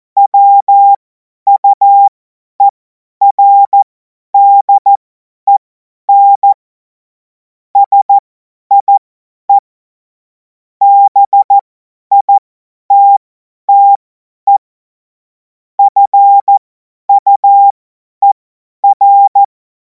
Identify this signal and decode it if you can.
Morse code, heard as a keyed tone, about 14 wpm, spaced as for 10 wpm, 800 Hz. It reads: WUERDEN SIE BITTE FUER
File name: Morsekurs / Witze / Witz 17. März 2026 50 F70